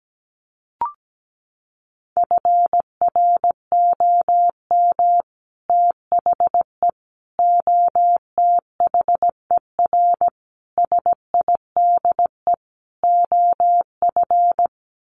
Morse code, heard as a keyed tone, about 17 wpm, 700 Hz. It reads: FROM THE OTHER SIDE OF
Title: Sets of 5 Words Encore 17wpm